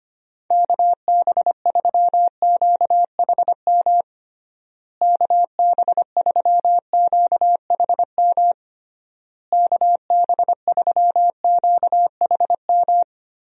Morse code, heard as a keyed tone, about 25 wpm, 700 Hz. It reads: KB3QHM KB3QHM KB3QHM
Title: USA Call Signs - 6 Characters - Repeated 3x - 25wpm